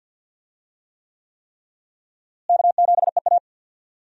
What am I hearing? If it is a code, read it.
Morse code, heard as a keyed tone, about 50 wpm, 700 Hz. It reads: K7EA
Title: USA Call Signs - 4 Characters - Repeated 3x - 50wpm